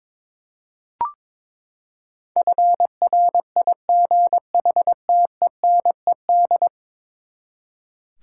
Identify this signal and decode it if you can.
Morse code, heard as a keyed tone, about 21 wpm, 700 Hz. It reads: FRIGHTENED
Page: morse-code-ninja-practice-22wpm